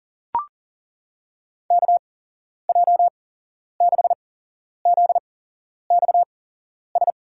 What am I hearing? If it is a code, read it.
Morse code, heard as a keyed tone, about 38 wpm, spaced as for 11 wpm, 700 Hz. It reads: KJ6ZXS
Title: USA Call Signs - 6 Characters - 8x Character Spacing - 40wpm